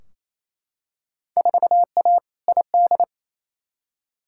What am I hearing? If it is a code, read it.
Morse code, heard as a keyed tone, about 28 wpm, 700 Hz. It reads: 4A ID
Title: Field Day Exchanges Encore - 28wpm